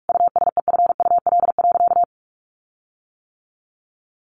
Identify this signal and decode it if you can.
Morse code, heard as a keyed tone, about 44 wpm, 700 Hz. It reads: USEFUL.
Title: Sentences from Top 1400 Words 45wpm